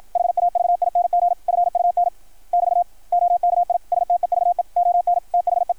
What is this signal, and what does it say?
Morse code, heard as a keyed tone, about 53 wpm, 690 Hz. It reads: HAVING FUN <BT> OKI STEVE ON THE
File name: archived / CRSnet